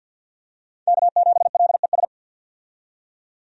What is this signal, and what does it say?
Morse code, heard as a keyed tone, about 53 wpm, 700 Hz. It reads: K7LES
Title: USA Call Signs - 5 Characters - 50wpm